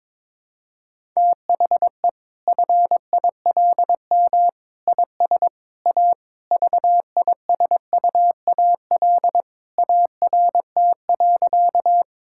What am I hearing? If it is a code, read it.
Morse code, heard as a keyed tone, about 22 wpm, 700 Hz. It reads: THE FILM IS A VISUAL ART.
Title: Sentences from Top 1000 Words - Review Part 6 - 22wpm